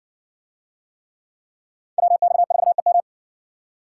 Morse code, heard as a keyed tone, about 56 wpm, 700 Hz. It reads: WX4EM